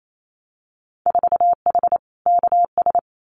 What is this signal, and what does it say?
Morse code, heard as a keyed tone, about 28 wpm, 700 Hz. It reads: 4H KS
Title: Winter Field Day Exchanges Encore 28wpm